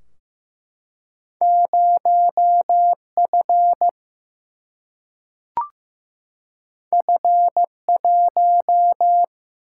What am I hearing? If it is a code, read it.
Morse code, heard as a keyed tone, about 16 wpm, 700 Hz. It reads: 0F F1
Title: Single-Letter Single-Number 15wpm